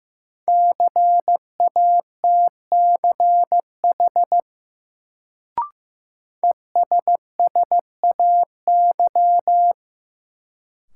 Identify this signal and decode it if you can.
Morse code, heard as a keyed tone, about 15 wpm, 700 Hz. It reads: CATCH ESSAY
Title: Sentences from Top 1500 Words - New Word Drill Alt - 15wpm